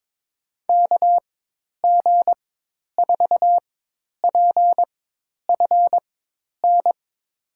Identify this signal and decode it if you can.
Morse code, heard as a keyed tone, about 21 wpm, spaced as for 10 wpm, 700 Hz. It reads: KG4PFN
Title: USA Call Signs - 6 Characters - 4x Character Spacing - 22wpm